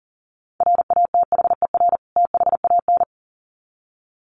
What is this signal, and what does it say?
Morse code, heard as a keyed tone, about 40 wpm, 700 Hz. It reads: RATHER THAN
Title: Sets of 2 Words 40wpm